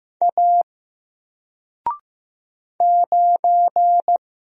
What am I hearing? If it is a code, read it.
Morse code, heard as a keyed tone, about 16 wpm, 700 Hz. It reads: A 9
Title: Single Letter-Number 15wpm